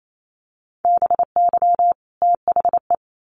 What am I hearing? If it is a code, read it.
Morse code, heard as a keyed tone, about 28 wpm, 700 Hz. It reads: BY THE
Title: Sets of 2 Words 28wpm